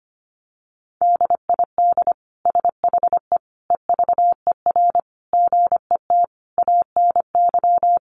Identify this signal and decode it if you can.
Morse code, heard as a keyed tone, about 25 wpm, 700 Hz. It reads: DID SHE EVER GET ANY
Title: Sentences from Top 1100 Words 25wpm